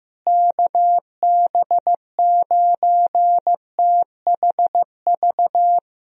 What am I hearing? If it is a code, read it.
Morse code, heard as a keyed tone, about 15 wpm, 700 Hz. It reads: KB9THV
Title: USA Call Signs - 6 Characters - Repeated 2x - 15wpm